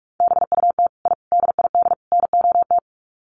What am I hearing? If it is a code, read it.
Morse code, heard as a keyed tone, about 44 wpm, 700 Hz. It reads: BUT I DID NOT
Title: Sets of 4 Words 45wpm